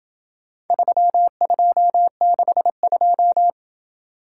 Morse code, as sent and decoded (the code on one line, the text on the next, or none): ...-- ..--- -.... ..---
3262